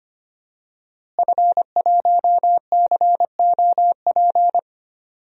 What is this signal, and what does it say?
Morse code, heard as a keyed tone, about 25 wpm, 700 Hz. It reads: F1COP
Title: DXCC-100 Call Signs - 5 Characters - Repeated 3x - 25wpm